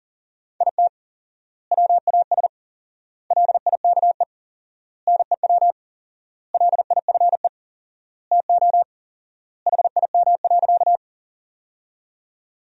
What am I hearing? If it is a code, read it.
Morse code, heard as a keyed tone, about 40 wpm, 700 Hz. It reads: IT WAS LIKE NEW LIFE TO HIM.
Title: 4x Word-Spacing - Sentences from Top 300 Words - 40wpm